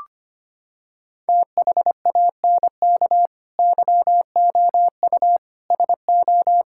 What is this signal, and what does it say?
Morse code, heard as a keyed tone, about 25 wpm, 700 Hz. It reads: THANK YOU SO
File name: Sets of 9 Words - 25wpm